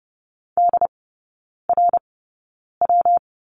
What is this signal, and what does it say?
Morse code, heard as a keyed tone, about 30 wpm, spaced as for 9 wpm, 700 Hz. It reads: DRW
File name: Lesson 19e - Letter w - Three Characters 30wpm